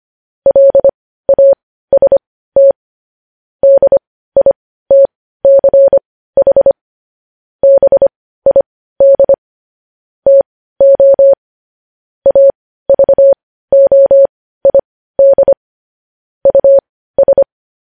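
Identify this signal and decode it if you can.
Morse code, heard as a keyed tone, about 25 wpm, spaced as for 15 wpm, 550 Hz. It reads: LAST DITCH BID TO AVOID US
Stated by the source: Morse code news headlines at 15 wpm.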